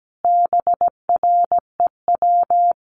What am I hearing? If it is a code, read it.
Morse code, heard as a keyed tone, about 17 wpm, 700 Hz. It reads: BREW
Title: Common QSO Phrases 17wpm